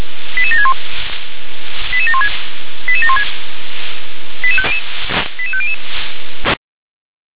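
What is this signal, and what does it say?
Five-tone selective call (ZVEI), heard as quick runs of five tones, tones of about 70 ms. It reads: 88761 88716 88716 88500
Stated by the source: ZVEI five-tone selective call, code 88799, received on an SDR.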